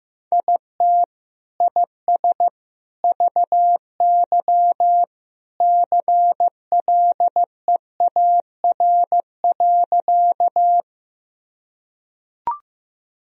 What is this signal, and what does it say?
Morse code, heard as a keyed tone, about 15 wpm, 700 Hz. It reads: IT IS VY CLEAR.
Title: Sentences from Top 500 Words and Abbreviations 15wpm